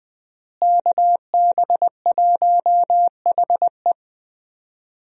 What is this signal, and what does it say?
Morse code, heard as a keyed tone, about 20 wpm, 700 Hz. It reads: KB1HE
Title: USA Call Signs - 5 Characters - Repeated 3x - 20wpm